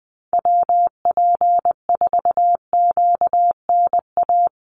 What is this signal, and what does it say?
Morse code, heard as a keyed tone, about 20 wpm, 700 Hz. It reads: WP4QNA
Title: USA Call Signs Repeated 3x 20wpm